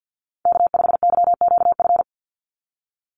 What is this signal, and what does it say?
Morse code, heard as a keyed tone, about 50 wpm, 700 Hz. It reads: K5YQF
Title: Call Signs - 50wpm